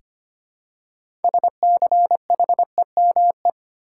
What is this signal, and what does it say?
Morse code, heard as a keyed tone, about 25 wpm, 700 Hz. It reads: SCHEME
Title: Six Letter Words - 25wpm